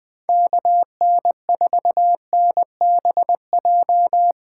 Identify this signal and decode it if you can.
Morse code, heard as a keyed tone, about 21 wpm, 700 Hz. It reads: KN4NBJ
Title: 2023 POTA Activator Call Signs - 20wpm